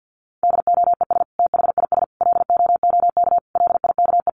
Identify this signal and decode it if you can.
Morse code, heard as a keyed tone, about 50 wpm, 700 Hz. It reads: DOES THIS LOOK LIKE
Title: Sentences from Top 1500 Words 50wpm